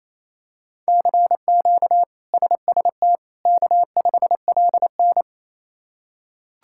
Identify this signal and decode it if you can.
Morse code, heard as a keyed tone, about 28 wpm, 700 Hz. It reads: CQ SST K5LN
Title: K1USN SST QSO Practice - 28wpm